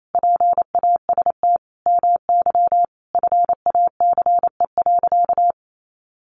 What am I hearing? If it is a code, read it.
Morse code, heard as a keyed tone, about 28 wpm, 700 Hz. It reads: PAST MY FACE.
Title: Sentences from Top 1100 Words 28wpm